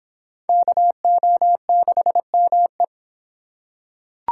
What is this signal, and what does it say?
Morse code, heard as a keyed tone, about 25 wpm, 700 Hz. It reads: KO6ME